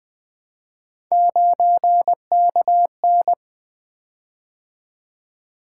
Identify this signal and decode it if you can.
Morse code, heard as a keyed tone, about 20 wpm, 700 Hz. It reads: 9KN
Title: Three Letter-Number Combinations - 20wpm